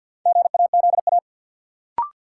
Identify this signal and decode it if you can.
Morse code, heard as a keyed tone, about 50 wpm, 700 Hz. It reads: GAZA
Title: Sets of 5 Words - New Word Drill - 50wpm